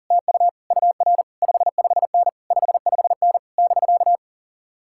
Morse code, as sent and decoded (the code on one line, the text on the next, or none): - ..- / ..- .-. / ..... ..... -. / ..... ..... -. / -...-.-
TU UR 55N 55N <BK>